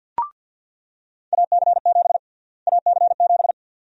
Morse code, heard as a keyed tone, about 46 wpm, 700 Hz. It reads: AC7 AC7
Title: USA Call Sign Prefixes - Repeated 2x - 50wpm